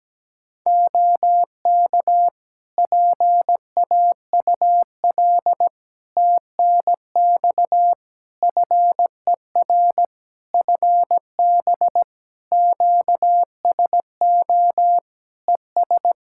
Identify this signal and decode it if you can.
Morse code, heard as a keyed tone, about 17 wpm, 700 Hz. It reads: OK PAUL TNX FER FB QSO ES
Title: K1USN SST QSO Practice 17wpm